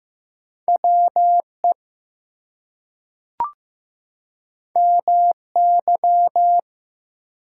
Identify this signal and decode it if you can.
Morse code, heard as a keyed tone, about 15 wpm, 700 Hz. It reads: WE MY
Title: Two Letter Words - 15wpm